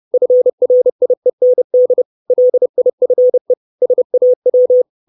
Morse code, heard as a keyed tone, about 30 wpm, 500 Hz. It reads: FRIEND LIFE SAW
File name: 632 words / 632 Words 30WPM